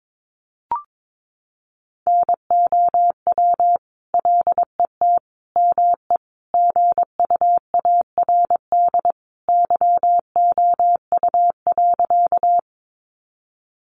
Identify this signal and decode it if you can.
Morse code, heard as a keyed tone, about 22 wpm, 700 Hz. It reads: NOW LET ME GUARD YOU.
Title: Sentences from Top 700 Words 22wpm